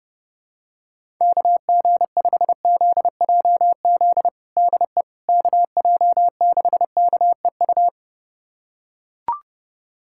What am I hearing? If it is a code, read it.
Morse code, heard as a keyed tone, about 31 wpm, 700 Hz. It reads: KG5ZJZ DE KJ6KEU
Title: USA Callsign DE Callsign - 6 Characters - 30wpm